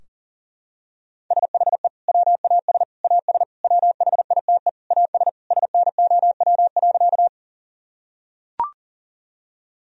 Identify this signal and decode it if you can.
Morse code, heard as a keyed tone, about 40 wpm, 700 Hz. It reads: SHE WAS AS WHITE AS SNOW.
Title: Sentences from Top 1000 Words - Review Part 3 - 40wpm